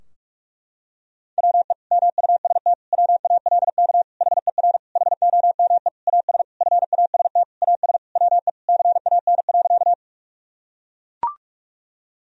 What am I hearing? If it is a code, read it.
Morse code, heard as a keyed tone, about 41 wpm, 700 Hz. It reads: WE MUST WALK HER HOME AS FAST AS WE CAN.
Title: Sentences from Top 1000 Words - Review Part 2 - 45wpm